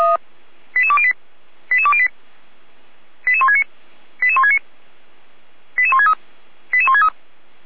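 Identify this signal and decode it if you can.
Five-tone selective call (ZVEI), heard as quick runs of five tones, tones of about 70 ms. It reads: 88208 88208 88169 88169 88163 88163
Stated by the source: ZVEI five-tone selective call received by SDR, from fire brigade Feuerwehr Orlamünde audio server.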